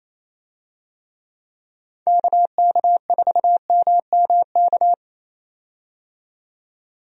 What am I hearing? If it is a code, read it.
Morse code, heard as a keyed tone, about 28 wpm, 700 Hz. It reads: KK4MMK
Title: USA Call Signs - 6 Characters - 28wpm